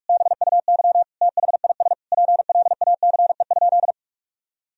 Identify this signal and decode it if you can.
Morse code, heard as a keyed tone, about 46 wpm, 700 Hz. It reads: BUY THIS PLACE?
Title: Sentences from Top 1000 Words - Review Part 4 - 45wpm